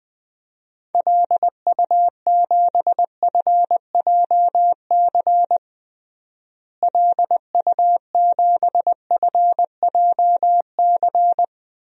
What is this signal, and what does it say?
Morse code, heard as a keyed tone, about 20 wpm, 700 Hz. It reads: LU7FJC LU7FJC
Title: DXCC-100 Call Signs - 6 Characters - Repeated 3x - 20wpm